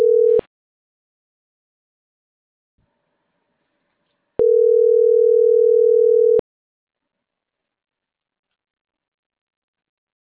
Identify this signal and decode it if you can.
Ringback tone (North America), 440 + 480 Hz.